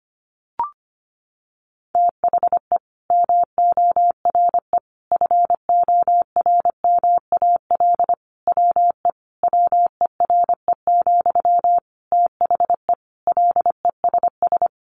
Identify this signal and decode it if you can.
Morse code, heard as a keyed tone, about 25 wpm, 700 Hz. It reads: THE MORE FORMAL WE WERE, THE LESS
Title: Sentences from Top 1300 Words 25wpm